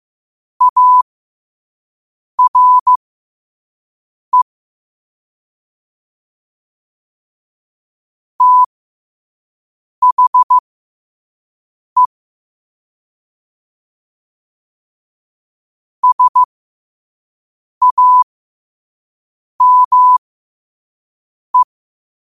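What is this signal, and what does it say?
Morse code, heard as a keyed tone, about 15 wpm, spaced as for 5 wpm, 1000 Hz. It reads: ARE THE SAME
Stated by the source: Morse at 5 wpm.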